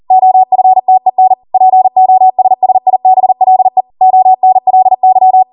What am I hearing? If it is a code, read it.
Morse code, heard as a keyed tone, about 40 wpm, 750 Hz. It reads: OFTEN POSSIBLE ONLY